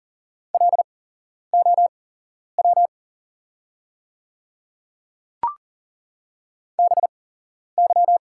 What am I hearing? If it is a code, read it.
Morse code, heard as a keyed tone, about 40 wpm, spaced as for 11 wpm, 690 Hz. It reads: LOW BY
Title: 8x Character Spacing - Top 200 Words - 40wpm